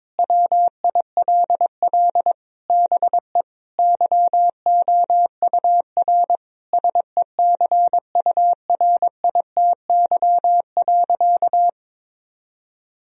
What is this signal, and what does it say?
Morse code, heard as a keyed tone, about 22 wpm, 700 Hz. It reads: WILL BE YOUR SECURITY.
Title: Sentences from Top 1000 Words - Review Part 4 - 22wpm